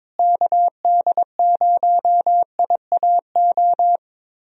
Morse code, heard as a keyed tone, about 22 wpm, 700 Hz. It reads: KD0IAO
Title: USA Call Signs - 6 Characters - Repeated 3x - 22wpm